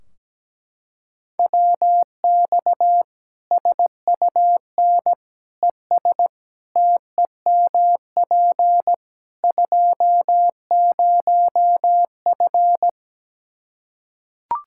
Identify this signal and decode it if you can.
Morse code, heard as a keyed tone, about 17 wpm, 700 Hz. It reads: WX SUN ES TEMP 20F
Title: Practice QSO - Exchange 2 - Info - 17wpm